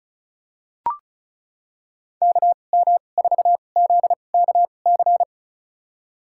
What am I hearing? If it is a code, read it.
Morse code, heard as a keyed tone, about 36 wpm, spaced as for 25 wpm, 700 Hz. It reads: KM4ZKC
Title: USA Call Signs - 2x Character Spacing - 35wpm